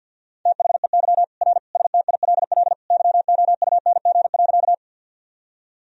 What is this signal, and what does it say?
Morse code, heard as a keyed tone, about 53 wpm, 700 Hz. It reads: THEY R STILL YOUNG.
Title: Sentences from Top 400 Words and Abbreviations 50wpm